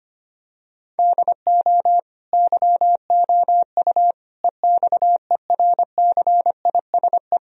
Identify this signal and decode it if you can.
Morse code, heard as a keyed tone, about 25 wpm, 700 Hz. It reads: DO YOU EXERCISE
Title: Sentences from Top 1000 Words - Review Part 4 - 25wpm